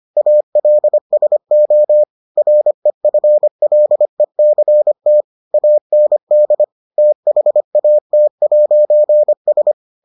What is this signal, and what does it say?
Morse code, heard as a keyed tone, about 25 wpm, 600 Hz. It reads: ALSO REFLECT AND THAT'S